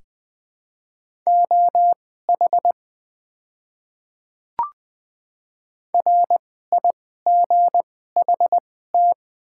Morse code, heard as a keyed tone, about 20 wpm, spaced as for 14 wpm, 700 Hz. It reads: OH RIGHT
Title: 2x Character Spacing - Top 500 Words - 20wpm